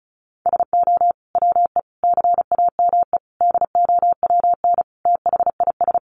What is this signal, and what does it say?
Morse code, heard as a keyed tone, about 34 wpm, 700 Hz. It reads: SO WE CAME DOWN THIS
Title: Sentences from Top 400 Words 35wpm